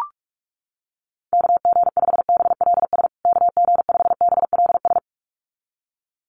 Morse code, heard as a keyed tone, about 44 wpm, 700 Hz. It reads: KG5BLS KG5BLS
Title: USA Call Signs Repeated 2x 45wpm